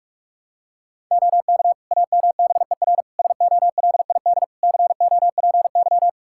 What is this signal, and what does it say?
Morse code, heard as a keyed tone, about 46 wpm, 700 Hz. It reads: OK AMBER SOLID COPY
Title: Practice QSO - Exchange 3 - More Info - 45wpm